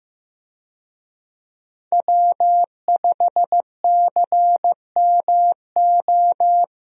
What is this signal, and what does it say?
Morse code, heard as a keyed tone, about 15 wpm, 700 Hz. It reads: W5CMO
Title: USA Call Signs - 5 Characters - Repeated 2x - 15wpm